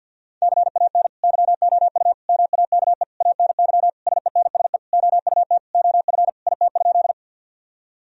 Morse code, heard as a keyed tone, about 50 wpm, 700 Hz. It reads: CAN YOU MAKE ANY SENSE OUT OF IT?
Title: Sentences from Top 1000 Words - Review Part 4 - 50wpm